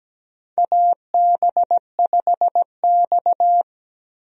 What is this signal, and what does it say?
Morse code, heard as a keyed tone, about 17 wpm, 700 Hz. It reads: AB5X
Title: USA Call Signs - 4 Characters - 17wpm